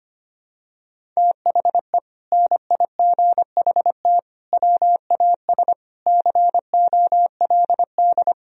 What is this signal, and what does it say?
Morse code, heard as a keyed tone, about 25 wpm, 700 Hz. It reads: THE NIGHT WAS COLD
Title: Sentences from Top 400 Words and Abbreviations 25wpm